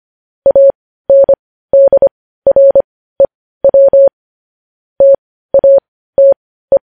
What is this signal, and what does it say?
Morse code, heard as a keyed tone, about 25 wpm, spaced as for 15 wpm, 550 Hz. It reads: ANDREW TATE
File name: podcast / News headlines-1x-25x15